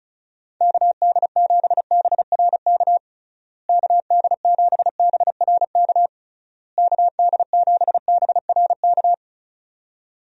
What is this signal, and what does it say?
Morse code, heard as a keyed tone, about 36 wpm, 700 Hz. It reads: KD7BRK KD7BRK KD7BRK
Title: USA Call Signs - 6 Characters - Repeated 3x - 35wpm